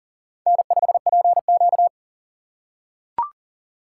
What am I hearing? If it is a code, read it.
Morse code, heard as a keyed tone, about 40 wpm, 700 Hz. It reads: N5PQ
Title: USA Call Signs - 4 Characters - 4x Character Spacing - 40wpm